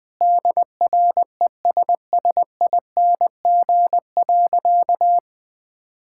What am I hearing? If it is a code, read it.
Morse code, heard as a keyed tone, about 20 wpm, 700 Hz. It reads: DRESSING.
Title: All-Age Appropriate Jokes - 20wpm